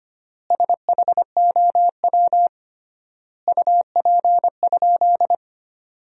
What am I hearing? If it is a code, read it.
Morse code, heard as a keyed tone, about 25 wpm, 700 Hz. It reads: SHOW UP?
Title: 3x Word-Spacing - Sentences from Top 200 Words - 25wpm